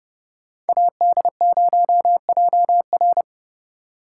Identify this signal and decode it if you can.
Morse code, heard as a keyed tone, about 34 wpm, 700 Hz. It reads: AD0JR
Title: Speed Racing - USA 2x2 Call Signs 30wpm